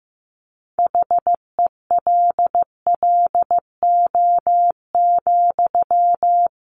Morse code, heard as a keyed tone, about 15 wpm, 700 Hz. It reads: HELLO,
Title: Sentences from Top 1100 Words 15wpm